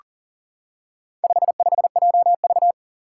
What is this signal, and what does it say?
Morse code, heard as a keyed tone, about 40 wpm, 700 Hz. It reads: 55JV
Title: Four Challenging Letter-Number Combinations 40wpm